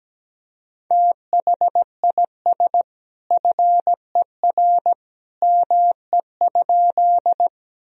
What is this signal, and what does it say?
Morse code, heard as a keyed tone, about 17 wpm, 700 Hz. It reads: THIS FER ME?
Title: Sentences from Top 200 Words and Abbreviations 17wpm